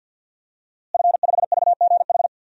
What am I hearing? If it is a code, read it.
Morse code, heard as a keyed tone, about 52 wpm, 700 Hz. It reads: F5VGH
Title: DXCC-100 Call Signs - 5 Characters - Repeated 3x - 50wpm